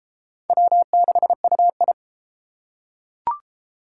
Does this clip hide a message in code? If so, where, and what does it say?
Yes: Morse code from 0.5 to 1.9 s, heard as a keyed tone, about 28 wpm, 700 Hz. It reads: W6UI